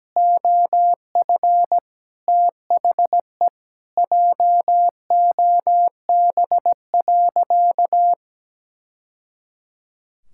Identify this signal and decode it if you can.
Morse code, heard as a keyed tone, about 17 wpm, 700 Hz. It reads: OF THE JOB.
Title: Sentences from Top 1000 Words - Review Part 5 - 17wpm